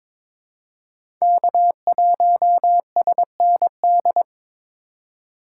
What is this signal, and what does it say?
Morse code, heard as a keyed tone, about 22 wpm, 700 Hz. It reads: K1SND
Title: K1USN SST QSO Practice - 22wpm